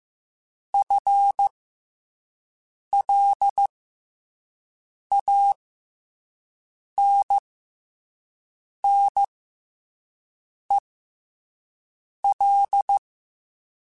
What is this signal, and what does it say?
Morse code, heard as a keyed tone, about 15 wpm, spaced as for 5 wpm, 800 Hz. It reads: FLANNEL
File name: FL Runs / FL Runs mp3s / FL Runs @5